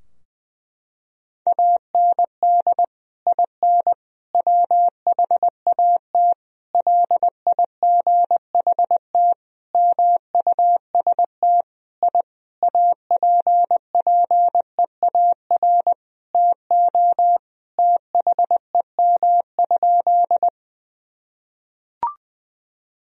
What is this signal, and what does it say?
Morse code, heard as a keyed tone, about 20 wpm, 700 Hz. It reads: AND IN WHAT LIGHT MUST I APPEAR TO THEM?
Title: Sentences from Top 1000 Words - Review Part 2 - 20wpm